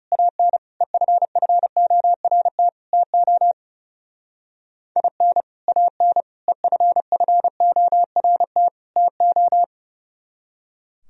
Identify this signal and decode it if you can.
Morse code, heard as a keyed tone, about 40 wpm, 700 Hz. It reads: AN EFFORT TO IN AN EFFORT TO